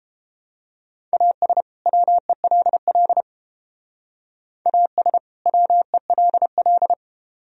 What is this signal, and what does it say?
Morse code, heard as a keyed tone, about 29 wpm, 700 Hz. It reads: AS WELL AS WELL